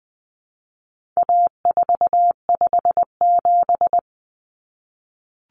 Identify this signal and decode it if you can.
Morse code, heard as a keyed tone, about 20 wpm, 700 Hz. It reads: A457
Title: Common Cut Numbers - 4-Digit 20wpm